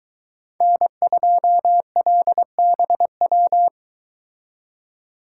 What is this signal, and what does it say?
Morse code, heard as a keyed tone, about 22 wpm, 700 Hz. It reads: N2LBW